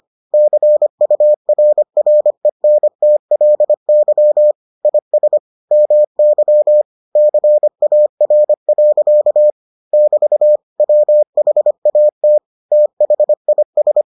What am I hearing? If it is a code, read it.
Morse code, heard as a keyed tone, about 25 wpm, 600 Hz. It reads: CURRENTLY IS MY CAR. <BT> WHAT THIS